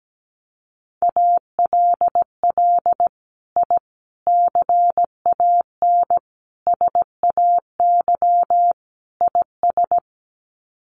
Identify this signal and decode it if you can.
Morse code, heard as a keyed tone, about 17 wpm, 700 Hz. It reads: ALL I CAN SAY IS
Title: Sets of 5 Words Encore2 17wpm